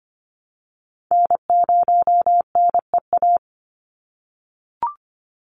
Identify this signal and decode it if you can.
Morse code, heard as a keyed tone, about 25 wpm, 700 Hz. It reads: N0NEA
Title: USA Call Signs 25wpm